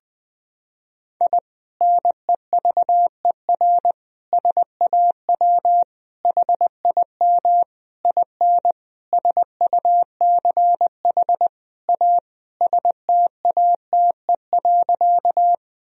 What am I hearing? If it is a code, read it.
Morse code, heard as a keyed tone, about 20 wpm, 700 Hz. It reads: I NEVER SAW HIM IN SUCH A STATE.
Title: Sentences from Top 1000 Words - Review Part 2 - 20wpm